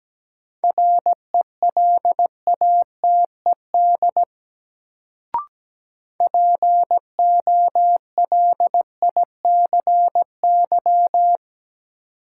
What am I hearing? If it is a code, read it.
Morse code, heard as a keyed tone, about 17 wpm, 700 Hz. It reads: RELATED POLICY
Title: Sentences from Top 1000 Words - Review Part 5 - New Word Drill - 17wpm